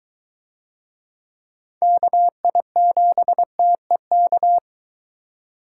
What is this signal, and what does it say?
Morse code, heard as a keyed tone, about 22 wpm, 700 Hz. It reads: KI7TEK